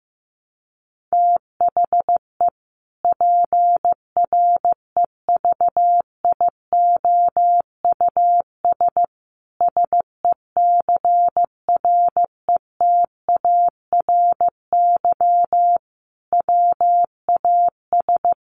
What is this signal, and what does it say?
Morse code, heard as a keyed tone, about 15 wpm, 700 Hz. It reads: THE PREVIOUS SECRETARY WAS
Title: Sentences from Top 1500 Words 15wpm